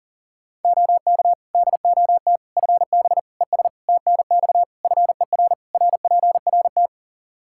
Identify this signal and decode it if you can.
Morse code, heard as a keyed tone, about 40 wpm, 700 Hz. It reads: OK DOT FB ES TNX FER RPRT
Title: Practice QSO - Exchange 2 - Info - 40wpm